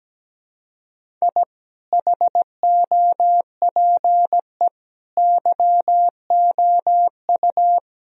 Morse code, heard as a keyed tone, about 17 wpm, 700 Hz. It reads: I HOPE YOU
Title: Sentences from Top 1000 Words - Review Part 1 - 17wpm